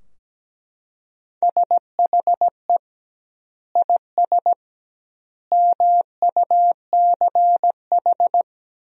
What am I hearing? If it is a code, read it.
Morse code, heard as a keyed tone, about 17 wpm, 700 Hz. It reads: SHE IS MUCH
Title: 2x Word-Spacing - Sentences from Top 400 Words - 17wpm